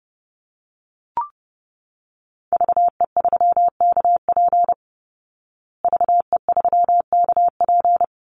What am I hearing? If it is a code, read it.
Morse code, heard as a keyed tone, about 30 wpm, 700 Hz. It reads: VE3KP VE3KP
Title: Call Signs Repeated 3x - 30wpm